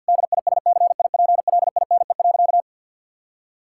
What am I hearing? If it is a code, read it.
Morse code, heard as a keyed tone, about 52 wpm, 700 Hz. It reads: DISCIPLINE.
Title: Sentences from Top 1000 Words - Review Part 6 - 50wpm